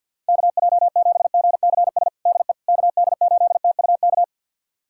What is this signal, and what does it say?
Morse code, heard as a keyed tone, about 51 wpm, 700 Hz. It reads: KJ7GCS DE KD8TUK
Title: USA Callsign DE Callsign 50wpm